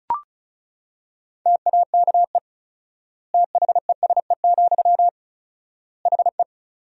Morse code, heard as a keyed tone, about 34 wpm, 700 Hz. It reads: TAKE THESE, HE
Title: 4x Word-Spacing - Sentences from Top 300 Words - 35wpm